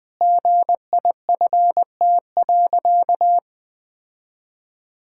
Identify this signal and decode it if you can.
Morse code, heard as a keyed tone, about 21 wpm, 700 Hz. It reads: GIFT.
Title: Sentences from Top 1000 Words - Review Part 6 - 20wpm